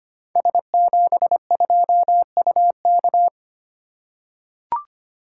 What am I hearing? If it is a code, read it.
Morse code, heard as a keyed tone, about 25 wpm, 700 Hz. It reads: S72UK
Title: Code Groups - 5 Letter:Numbers - 25wpm